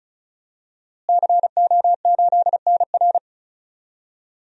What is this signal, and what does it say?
Morse code, heard as a keyed tone, about 36 wpm, 700 Hz. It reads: CO8NR
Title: DXCC-100 Call Signs - 5 Characters - Repeated 3x - 35wpm